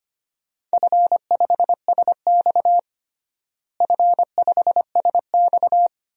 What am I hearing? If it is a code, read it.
Morse code, heard as a keyed tone, about 25 wpm, 700 Hz. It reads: F5SX F5SX
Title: DXCC-100 Call Signs - 4 Characters - Repeated 3x - 25wpm